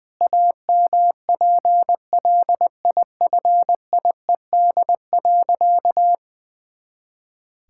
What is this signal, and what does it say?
Morse code, heard as a keyed tone, about 20 wpm, 700 Hz. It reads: AMPLIFIED.
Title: Pangram Challenge v2 - 20wpm